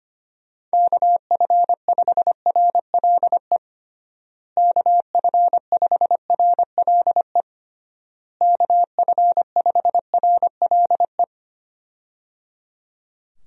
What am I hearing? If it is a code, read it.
Morse code, heard as a keyed tone, about 25 wpm, 700 Hz. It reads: KF5RLE KF5RLE KF5RLE
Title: USA Call Signs - 6 Characters - Repeated 3x - 25wpm